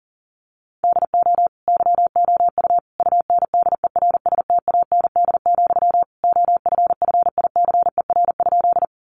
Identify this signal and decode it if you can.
Morse code, heard as a keyed tone, about 40 wpm, 700 Hz. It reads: DO YOU UNDERSTAND, OFFICER?
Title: Sentences from Top 1200 Words 40wpm